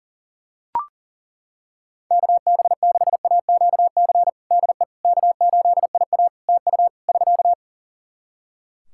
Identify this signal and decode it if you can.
Morse code, heard as a keyed tone, about 40 wpm, 700 Hz. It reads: KB6AQC DE K8IA TU <SK>
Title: K1USN SST QSO Practice 40wpm